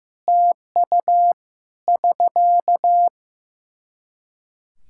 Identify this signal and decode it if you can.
Morse code, heard as a keyed tone, about 15 wpm, 700 Hz. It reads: TU <SK>
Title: Practice QSO - Exchange 4 - Ending - 15wpm